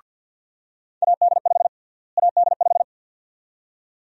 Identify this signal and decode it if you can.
Morse code, heard as a keyed tone, about 52 wpm, 700 Hz. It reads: AD5 AD5
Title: USA Call Sign Prefixes - Repeated 2x - 50wpm